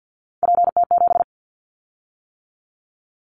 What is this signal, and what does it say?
Morse code, heard as a keyed tone, about 50 wpm, 700 Hz. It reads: PT7
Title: Callsign Prefixes for Top 50 DXCC - 50wpm